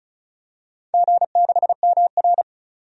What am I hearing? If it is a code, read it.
Morse code, heard as a keyed tone, about 36 wpm, 700 Hz. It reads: G6MR
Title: DXCC-100 Call Signs - 4 Characters - 35wpm